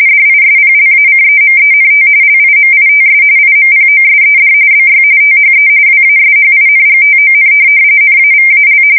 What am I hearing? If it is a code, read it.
RTTY, about 45.45 baud. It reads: DISPLACEMENT LIMITS IMPOSED BY THE WASHINGTON NAVAL
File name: fsk170 / USS-Missouri